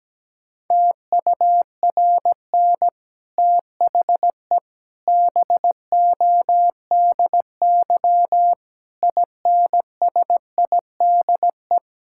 Morse code, heard as a keyed tone, about 17 wpm, 700 Hz. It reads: TURN THE BODY INSIDE
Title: Sentences from Top 1000 Words - Review Part 6 - 17wpm